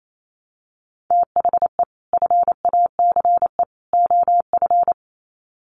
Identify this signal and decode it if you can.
Morse code, heard as a keyed tone, about 28 wpm, 700 Hz. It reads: THE FACE OF
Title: Sets of 3 Words 28wpm